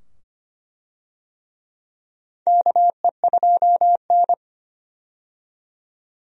Morse code, heard as a keyed tone, about 25 wpm, 700 Hz. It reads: KE2N
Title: USA Call Signs - 4 Characters - 25wpm